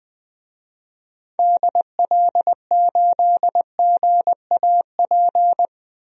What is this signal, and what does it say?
Morse code, heard as a keyed tone, about 21 wpm, 700 Hz. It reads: DL8GAP
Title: DXCC-100 Call Signs - 6 Characters - Repeated 2x - 20wpm